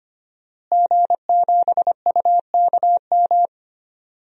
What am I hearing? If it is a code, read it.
Morse code, heard as a keyed tone, about 25 wpm, 700 Hz. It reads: G7UKM
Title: DXCC-100 Call Signs - 5 Characters - Repeated 2x - 25wpm